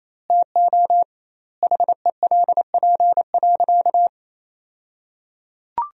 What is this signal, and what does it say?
Morse code, heard as a keyed tone, about 28 wpm, 700 Hz. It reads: TO HELP.
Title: 2x Word-Spacing - Sentences from Top 300 Words - 28wpm